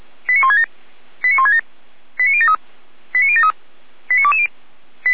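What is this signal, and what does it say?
Five-tone selective call (ZVEI), heard as quick runs of five tones, tones of about 80 ms. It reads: 89167 89167 89073 89073 89220
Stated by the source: ZVEI five-tone selective call 89123, received on SDR by Feuerwehr Orlamünde.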